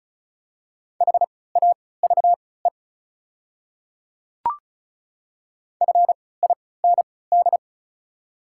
Morse code, heard as a keyed tone, about 34 wpm, spaced as for 20 wpm, 690 Hz. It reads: HAVE FIND
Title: 3x Character Spacing - Top 100 Words - 35wpm